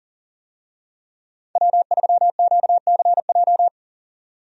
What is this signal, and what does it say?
Morse code, heard as a keyed tone, about 40 wpm, 700 Hz. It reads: W3QCJ
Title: USA Call Signs - 5 Characters - 40wpm